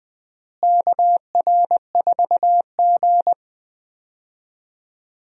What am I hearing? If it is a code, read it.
Morse code, heard as a keyed tone, about 21 wpm, 700 Hz. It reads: KR4G
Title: USA Call Signs - 4 Characters - Repeated 3x - 20wpm